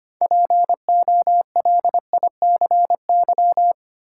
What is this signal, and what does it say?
Morse code, heard as a keyed tone, about 25 wpm, 700 Hz. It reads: POLICY
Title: Sentences from Top 1000 Words - Review Part 5 - New Word Drill - 25wpm